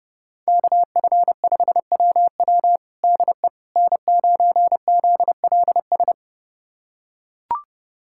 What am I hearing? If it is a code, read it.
Morse code, heard as a keyed tone, about 32 wpm, 700 Hz. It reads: KF5WW DE N9ZLS
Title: USA Callsign DE Callsign - 5 Characters - 30wpm